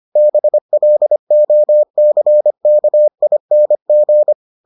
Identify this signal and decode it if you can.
Morse code, heard as a keyed tone, about 25 wpm, 600 Hz. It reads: BLOCKING